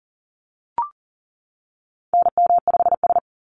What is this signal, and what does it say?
Morse code, heard as a keyed tone, about 40 wpm, 700 Hz. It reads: NM5S
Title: NA SOTA Chaser Callsigns 40wpm